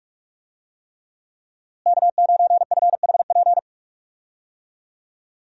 Morse code, heard as a keyed tone, about 46 wpm, 700 Hz. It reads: K9FHP
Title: USA Call Signs - 5 Characters - Repeated 3x - 45wpm